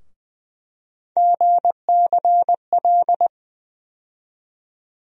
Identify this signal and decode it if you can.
Morse code, heard as a keyed tone, about 20 wpm, 700 Hz. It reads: GCL
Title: USA Call Sign Postfixes - 3 Letters 20wpm